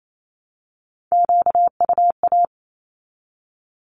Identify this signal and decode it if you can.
Morse code, heard as a keyed tone, about 28 wpm, 700 Hz. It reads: QUA
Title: Q-Codes 28wpm